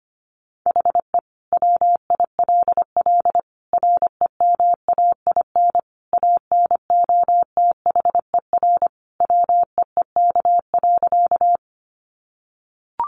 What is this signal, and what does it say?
Morse code, heard as a keyed tone, about 25 wpm, 700 Hz. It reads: HE WILL REMAIN ANOTHER WEEK.
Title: Sentences from Top 1200 Words 25wpm